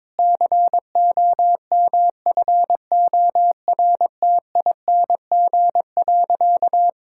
Morse code, heard as a keyed tone, about 22 wpm, 700 Hz. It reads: COMFORTING.
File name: Difficult Sentences with Hints - 22wpm